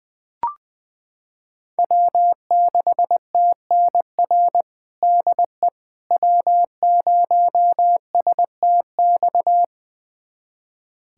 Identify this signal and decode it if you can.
Morse code, heard as a keyed tone, about 21 wpm, 700 Hz. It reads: W6TNR DE W0STX
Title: USA Callsign DE Callsign - 5 Characters - 20wpm